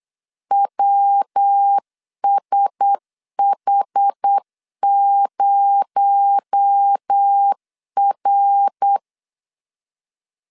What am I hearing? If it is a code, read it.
Morse code, heard as a keyed tone, about 8 wpm, 800 Hz. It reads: WSH0R
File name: LEZIONI / 7 GF9Z / 7 G 8wpm 3118